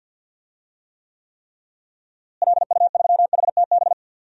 Morse code, heard as a keyed tone, about 47 wpm, 700 Hz. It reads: LU3HTB
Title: DXCC-100 Call Signs - 6 Characters - Repeated 3x - 50wpm